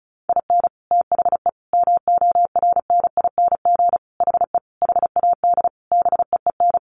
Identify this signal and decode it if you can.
Morse code, heard as a keyed tone, about 34 wpm, 700 Hz. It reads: IN THE MORNING HE HAD BEEN
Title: Sentences from Top 1200 Words 35wpm